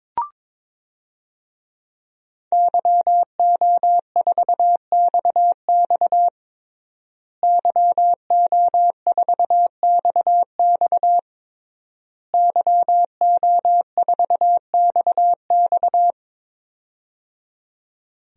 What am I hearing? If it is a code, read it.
Morse code, heard as a keyed tone, about 22 wpm, 700 Hz. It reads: YO4XX YO4XX YO4XX
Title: DXCC-100 Call Signs - 5 Characters - Repeated 3x - 22wpm